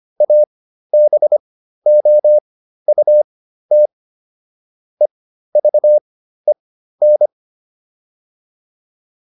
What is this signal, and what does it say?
Morse code, heard as a keyed tone, about 25 wpm, spaced as for 13 wpm, 600 Hz. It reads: ABOUT EVEN
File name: PR405 13